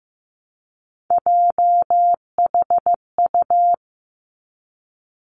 Morse code, heard as a keyed tone, about 15 wpm, 700 Hz. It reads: JHU
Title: Three Challenging Letter-Number Combinations 15wpm